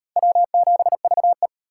Morse code, heard as a keyed tone, about 36 wpm, 700 Hz. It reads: W7VE